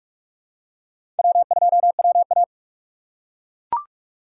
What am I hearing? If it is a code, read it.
Morse code, heard as a keyed tone, about 42 wpm, 690 Hz. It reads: W2WA
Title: USA Call Signs - 4 Characters - 4x Character Spacing - 45wpm